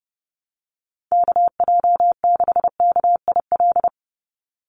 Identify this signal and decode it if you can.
Morse code, heard as a keyed tone, about 30 wpm, 700 Hz. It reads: KJ6KIL
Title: USA Call Signs 30wpm